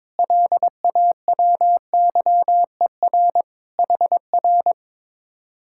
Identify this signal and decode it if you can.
Morse code, heard as a keyed tone, about 22 wpm, 700 Hz. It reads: LAWYER HR
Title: Speed-Racing Common QSO Phrases - 22wpm